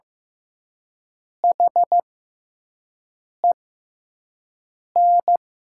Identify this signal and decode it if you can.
Morse code, heard as a keyed tone, about 15 wpm, spaced as for 5 wpm, 700 Hz. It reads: HEN
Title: 6x Character Spacing - Top 300 Words - 15wpm